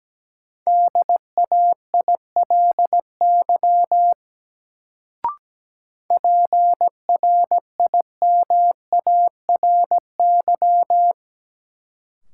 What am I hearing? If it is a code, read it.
Morse code, heard as a keyed tone, about 17 wpm, 700 Hz. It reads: DAILY PRIMARY
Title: Sentences from Top 1100 Words - New Word Drill Alt - 17wpm